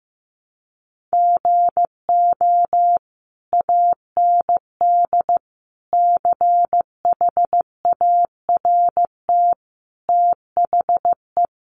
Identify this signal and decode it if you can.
Morse code, heard as a keyed tone, about 15 wpm, 700 Hz. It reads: GO AND CHART THE
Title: Sentences from Top 1300 Words 15wpm